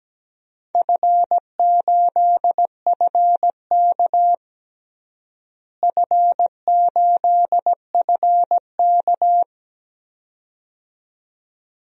Morse code, heard as a keyed tone, about 17 wpm, 700 Hz. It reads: F8FK F8FK
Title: DXCC-100 Call Signs - 4 Characters - Repeated 2x - 17wpm